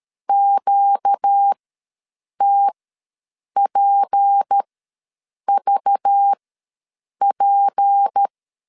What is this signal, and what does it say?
Morse code, heard as a keyed tone, about 13 wpm, spaced as for 7 wpm, 800 Hz. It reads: QTPVP
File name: LEZIONI / 6 PKX0 / 6 P ws 12wpm 3117